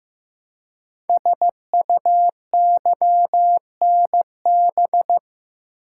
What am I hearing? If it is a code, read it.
Morse code, heard as a keyed tone, about 15 wpm, 700 Hz. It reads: SUYNB
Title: Code Groups - 5 Letters - 15wpm